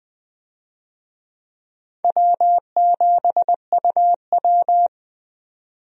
Morse code, heard as a keyed tone, about 20 wpm, 700 Hz. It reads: W7UW
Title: USA Call Signs - 4 Characters - 20wpm